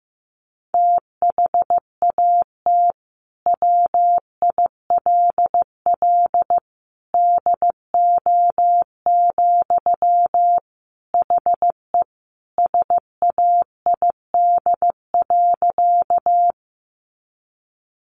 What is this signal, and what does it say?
Morse code, heard as a keyed tone, about 15 wpm, 700 Hz. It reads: THAT WILL DO, HE SAID.
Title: Sentences from Top 100 Words 15wpm